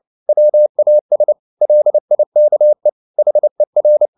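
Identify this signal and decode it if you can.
Morse code, heard as a keyed tone, about 29 wpm, 600 Hz. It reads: WAS LIKE HER
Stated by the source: Morse code at 29 wpm, sending Ken Follett's novel Never in English, from CW med Gnister.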